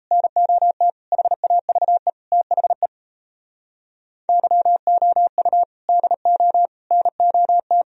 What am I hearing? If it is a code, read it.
Morse code, heard as a keyed tone, about 32 wpm, 700 Hz. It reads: NOT HAVE THE YOU DO NOT
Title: Speed-Racing - Sets of 5 Words 28wpm